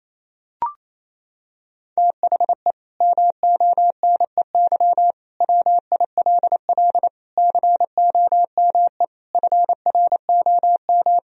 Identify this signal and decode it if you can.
Morse code, heard as a keyed tone, about 28 wpm, 700 Hz. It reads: THE MONEY WILL COME FROM
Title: Sentences from Top 1000 Words - Review Part 4 - 28wpm